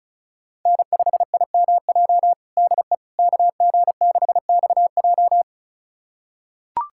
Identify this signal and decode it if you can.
Morse code, heard as a keyed tone, about 36 wpm, 700 Hz. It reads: N5IMJ DE KG6XJ
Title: USA Callsign DE Callsign - 5 Characters - 35wpm